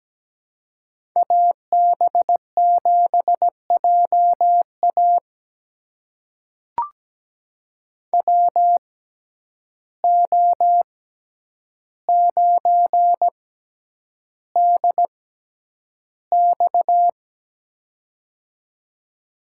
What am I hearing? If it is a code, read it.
Morse code, heard as a keyed tone, about 17 wpm, spaced as for 9 wpm, 700 Hz. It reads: AB7JA WO9DX
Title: USA Call Signs - 5 Characters - 6x Character Spacing - 17wpm